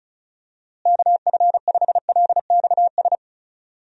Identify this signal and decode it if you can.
Morse code, heard as a keyed tone, about 34 wpm, 700 Hz. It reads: KF5LXS
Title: USA Call Signs - 6 Characters - Repeated 2x - 35wpm